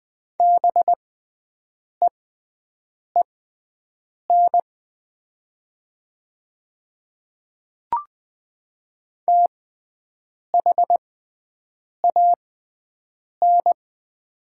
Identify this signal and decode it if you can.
Morse code, heard as a keyed tone, about 20 wpm, spaced as for 7 wpm, 700 Hz. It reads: BEEN THAN
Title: 6x Character Spacing - Top 300 Words - 20wpm